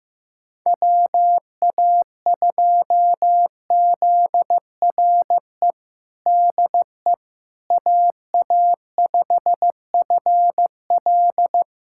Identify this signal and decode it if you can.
Morse code, heard as a keyed tone, about 15 wpm, 700 Hz. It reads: WA2ZRE DE AA5FL
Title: USA Callsign DE Callsign 15wpm